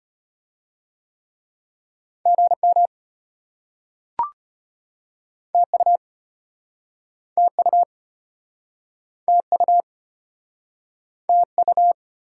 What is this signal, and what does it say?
Morse code, heard as a keyed tone, about 33 wpm, 700 Hz. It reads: GM TU TU TU TU